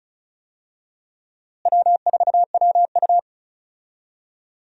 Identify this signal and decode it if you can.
Morse code, heard as a keyed tone, about 34 wpm, 700 Hz. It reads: W4WU
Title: USA Call Signs - 4 Characters - Repeated 2x - 35wpm